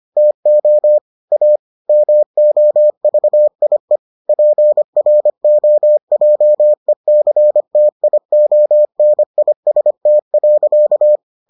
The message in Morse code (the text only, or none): TO A MOVIE PROJECTIONIST.